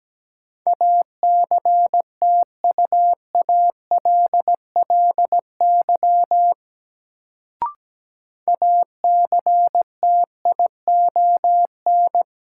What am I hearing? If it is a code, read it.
Morse code, heard as a keyed tone, about 17 wpm, 700 Hz. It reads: ACTUALLY ACTION
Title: Sentences from Top 700 Words - New Word Drill - 17wpm